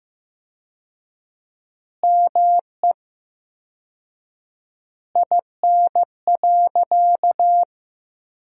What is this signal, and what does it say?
Morse code, heard as a keyed tone, about 15 wpm, 700 Hz. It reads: ME IN.
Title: 4x Word-Spacing - Sentences from Top 500 Words - 15wpm